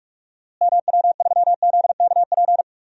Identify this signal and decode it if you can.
Morse code, heard as a keyed tone, about 47 wpm, 700 Hz. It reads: MW3ZKP
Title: DXCC-100 Call Signs - 6 Characters - Repeated 2x - 45wpm